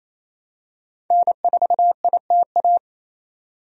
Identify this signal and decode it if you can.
Morse code, heard as a keyed tone, about 28 wpm, 700 Hz. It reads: N4ITA
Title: USA Call Signs - 5 Characters - 28wpm